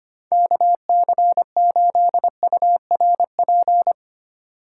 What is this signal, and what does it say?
Morse code, heard as a keyed tone, about 31 wpm, 700 Hz. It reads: KC8URP
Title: Speed Racing - USA 2x3 Call Signs 25wpm